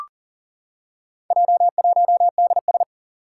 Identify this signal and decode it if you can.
Morse code, heard as a keyed tone, about 40 wpm, 700 Hz. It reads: J1DS
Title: Four Challenging Letter-Number Combinations 40wpm